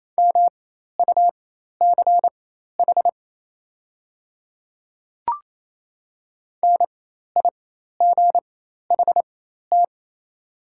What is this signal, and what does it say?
Morse code, heard as a keyed tone, about 28 wpm, spaced as for 13 wpm, 700 Hz. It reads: MUCH NIGHT
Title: 4x Character Spacing - Top 300 Words - 28wpm